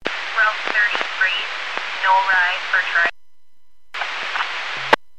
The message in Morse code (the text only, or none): none